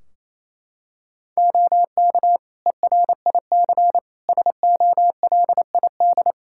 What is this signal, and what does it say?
Morse code, heard as a keyed tone, about 28 wpm, 700 Hz. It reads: OK ERIC SOLID
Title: Practice QSO - Exchange 3 - More Info - 28wpm